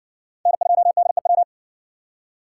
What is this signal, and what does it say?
Morse code, heard as a keyed tone, about 60 wpm, 700 Hz. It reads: N2DEW